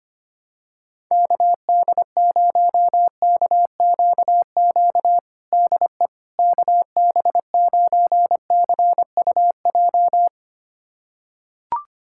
Morse code, heard as a keyed tone, about 25 wpm, 700 Hz. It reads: KD0KQQ DE KB9CUJ
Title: USA Callsign DE Callsign - 6 Characters - 25wpm